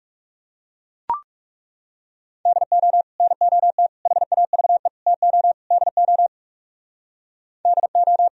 Morse code, heard as a keyed tone, about 38 wpm, 700 Hz. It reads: DO NOT HAVE TO DO DO